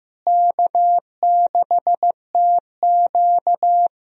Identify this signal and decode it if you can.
Morse code, heard as a keyed tone, about 15 wpm, 700 Hz. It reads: K6TQ
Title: USA Call Signs - 4 Characters - Repeated 3x - 15wpm